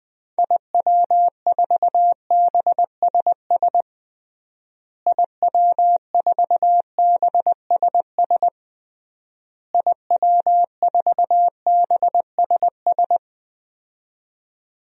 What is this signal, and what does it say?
Morse code, heard as a keyed tone, about 21 wpm, 700 Hz. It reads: IW4BSS IW4BSS IW4BSS
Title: DXCC-100 Call Signs - 6 Characters - Repeated 3x - 20wpm